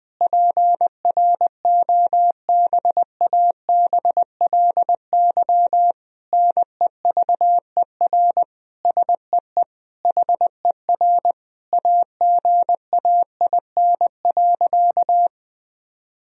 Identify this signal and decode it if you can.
Morse code, heard as a keyed tone, about 21 wpm, 700 Hz. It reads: PROBABLY NEVER SEE HER AGAIN.
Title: Sentences from Top 1000 Words - Review Part 4 - 20wpm